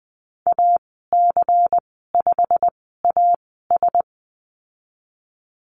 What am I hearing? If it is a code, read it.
Morse code, heard as a keyed tone, about 20 wpm, spaced as for 15 wpm, 700 Hz. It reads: AC5AS
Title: 154 Lesson 25g - Letter p - Call Signs 20wpm